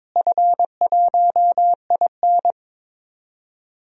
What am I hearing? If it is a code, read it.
Morse code, heard as a keyed tone, about 22 wpm, 700 Hz. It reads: F1IN
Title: DXCC-100 Call Signs - 4 Characters - Repeated 2x - 22wpm